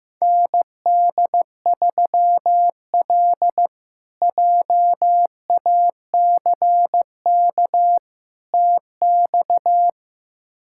Morse code, heard as a keyed tone, about 15 wpm, 700 Hz. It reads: ND3L JACK TX
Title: K1USN SST QSO Practice - 15wpm